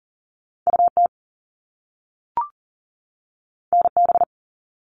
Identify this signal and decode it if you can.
Morse code, heard as a keyed tone, about 40 wpm, 700 Hz. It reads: UT NB
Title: All Cut Numbers - 2-Digit 40wpm